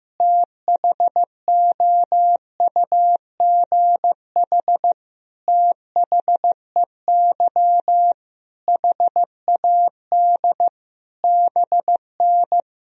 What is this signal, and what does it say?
Morse code, heard as a keyed tone, about 15 wpm, 700 Hz. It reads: THOUGH THEY HAD BN
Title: Sentences from Top 400 Words and Abbreviations 15wpm